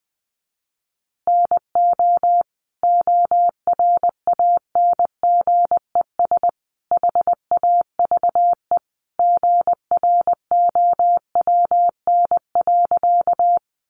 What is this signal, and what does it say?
Morse code, heard as a keyed tone, about 20 wpm, 700 Hz. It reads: NO ORANGES HAVE GROWN.
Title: Easy Phrases 20wpm